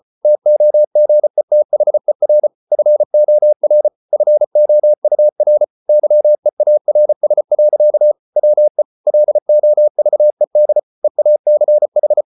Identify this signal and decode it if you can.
Morse code, heard as a keyed tone, about 34 wpm, 600 Hz. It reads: TOGETHER FOR FOUR YEARS. WE LOVED EACH